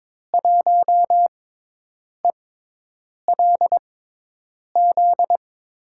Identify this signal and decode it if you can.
Morse code, heard as a keyed tone, about 22 wpm, spaced as for 8 wpm, 700 Hz. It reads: 1ELZ
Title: USA Call Signs - 6 Characters - 6x Character Spacing - 22wpm